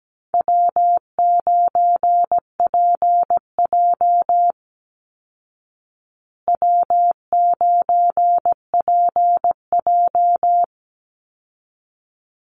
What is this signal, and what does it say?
Morse code, heard as a keyed tone, about 17 wpm, 700 Hz. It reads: W9PJ W9PJ
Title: Call Signs Repeated 3x - 17wpm